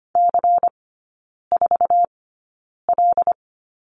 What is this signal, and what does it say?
Morse code, heard as a keyed tone, about 25 wpm, spaced as for 9 wpm, 700 Hz. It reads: C4L
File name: course / 25:10wpm / 093 Lesson 16e - Letter c - Three Characters 25wpm